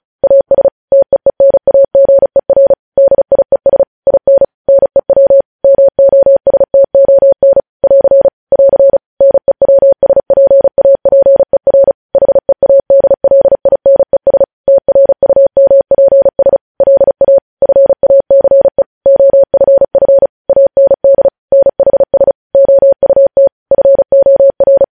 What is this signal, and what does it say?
Morse code, heard as a keyed tone, about 36 wpm, 550 Hz. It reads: AS TEENAGER DIES IN NEW MOSTON <AR> <AR> NEWSPAPER HEADLINES TRUMPS LA FACE OFF AND NHS OUT FOR
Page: news headlines in morse code at 35 wpm